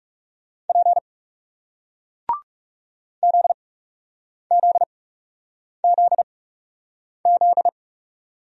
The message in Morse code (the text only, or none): P ZZZZ